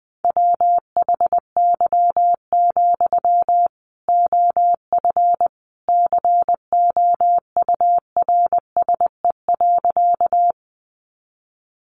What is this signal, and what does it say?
Morse code, heard as a keyed tone, about 20 wpm, 700 Hz. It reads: WHY, OF COURSE.
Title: Sentences from Top 500 Words 20wpm